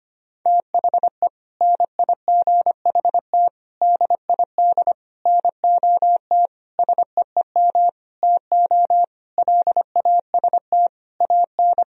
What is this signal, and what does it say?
Morse code, heard as a keyed tone, about 25 wpm, 700 Hz. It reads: THE NIGHT DID NOT SEEM TO LAST AN
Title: Sentences from Top 1000 Words - Review Part 1 - 25wpm